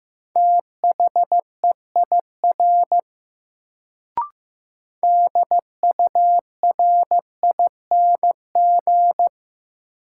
Morse code, heard as a keyed tone, about 15 wpm, 700 Hz. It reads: THEIR DURING
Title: Sets of 2 Words - New Word Drill - 15wpm